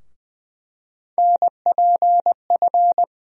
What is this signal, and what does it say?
Morse code, heard as a keyed tone, about 20 wpm, 700 Hz. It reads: NPF
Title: USA Call Sign Postfixes - 3 Letters 20wpm